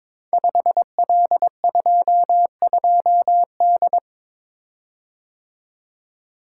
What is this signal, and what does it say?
Morse code, heard as a keyed tone, about 22 wpm, 700 Hz. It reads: 5L22D
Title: Code Groups - 5 Letter:Numbers - 22wpm